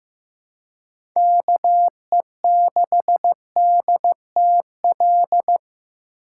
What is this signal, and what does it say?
Morse code, heard as a keyed tone, about 15 wpm, 700 Hz. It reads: KE6DTL
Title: USA Call Signs - 6 Characters - 2x Character Spacing - 15wpm